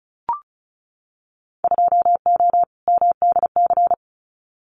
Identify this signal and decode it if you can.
Morse code, heard as a keyed tone, about 34 wpm, 700 Hz. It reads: 2O MDC
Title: Winter Field Day Exchanges 35wpm